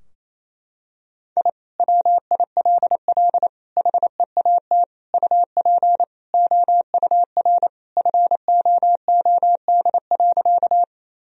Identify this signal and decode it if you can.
Morse code, heard as a keyed tone, about 28 wpm, 700 Hz. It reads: I WILL HEAT UP OUR FOOD.
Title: Sentences from Top 1000 Words - Review Part 3 - 28wpm